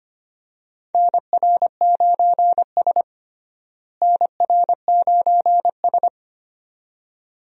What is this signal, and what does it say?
Morse code, heard as a keyed tone, about 25 wpm, 700 Hz. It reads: NR9S NR9S
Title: USA Call Signs - 4 Characters - Repeated 2x - 25wpm